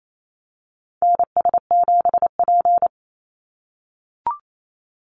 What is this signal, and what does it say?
Morse code, heard as a keyed tone, about 28 wpm, 700 Hz. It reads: NS7P
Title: NA SOTA Activator Callsigns 28wpm